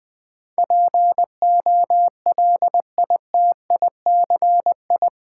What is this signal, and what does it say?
Morse code, heard as a keyed tone, about 21 wpm, 700 Hz. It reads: POLITICI
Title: 10 Letter Words - First 3 Letters Given - 20wpm